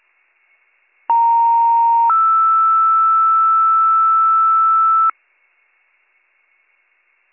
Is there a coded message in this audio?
Yes: two-tone page from 1.1 to 5.1 s, heard as a one-second tone, then a three-second tone, about 930 then 1320 Hz.